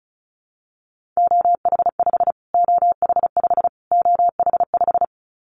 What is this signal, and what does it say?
Morse code, heard as a keyed tone, about 34 wpm, 700 Hz. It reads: OH5 OH5 OH5
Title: Callsign Prefixes for Top 25 DXCC - Repeated 3x - 35wpm